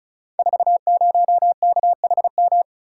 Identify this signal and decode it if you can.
Morse code, heard as a keyed tone, about 36 wpm, 700 Hz. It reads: 40KHM
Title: Code Groups - 5 Letter:Numbers - 35wpm